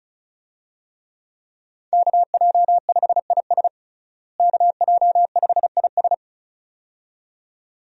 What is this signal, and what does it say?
Morse code, heard as a keyed tone, about 36 wpm, 700 Hz. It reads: KJ5IS KJ5IS
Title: USA Call Signs - 5 Characters - Repeated 2x - 35wpm